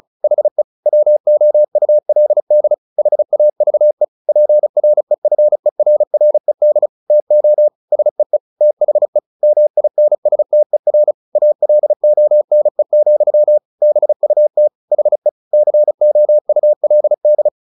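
Morse code, heard as a keyed tone, about 34 wpm, 600 Hz. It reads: HE WOULD HAVE PREFERRED TO SEE THE MINSTER ALONE, BUT HE COULD
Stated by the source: Morse practice at 35 wpm from CW med Gnister, sending Ken Follett's novel Never in English.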